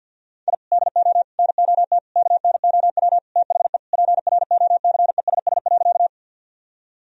Morse code, heard as a keyed tone, about 52 wpm, 700 Hz. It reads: I DO NOT KNOW THE PROCESS.
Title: Sentences from Top 1000 Words - Review Part 3 - 50wpm